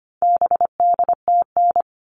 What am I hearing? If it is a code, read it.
Morse code, heard as a keyed tone, about 25 wpm, 700 Hz. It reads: BDTN
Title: All Cut Numbers - 4-Digit 25wpm